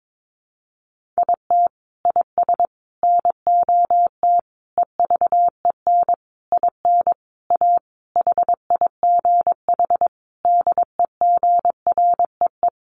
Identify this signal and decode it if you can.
Morse code, heard as a keyed tone, about 22 wpm, 700 Hz. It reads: IT IS NOT EVEN IN A HIGH DEGREE
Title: Sentences from Top 1500 Words 22wpm